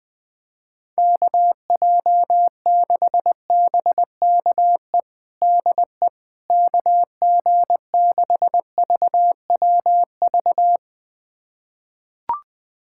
Morse code, heard as a keyed tone, about 20 wpm, 700 Hz. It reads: KJ6BKE DE KG6VWV
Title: USA Callsign DE Callsign - 6 Characters - 20wpm